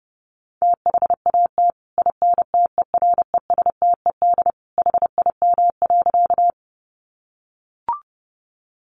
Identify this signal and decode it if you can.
Morse code, heard as a keyed tone, about 30 wpm, 700 Hz. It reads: THAT INTERESTED HIM.
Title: Sentences from Top 900 Words 30wpm